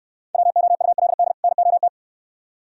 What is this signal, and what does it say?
Morse code, heard as a keyed tone, about 69 wpm, 700 Hz. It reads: WOULD NOT